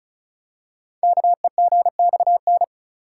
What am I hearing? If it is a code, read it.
Morse code, heard as a keyed tone, about 36 wpm, 700 Hz. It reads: KEGXN
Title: Code Groups - 5 Letters - 35wpm